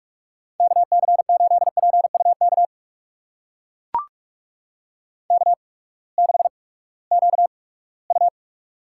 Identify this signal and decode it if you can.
Morse code, heard as a keyed tone, about 44 wpm, 690 Hz. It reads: K C 8 P U K K 6 Q U
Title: USA Call Signs - 8x Character Spacing - 45wpm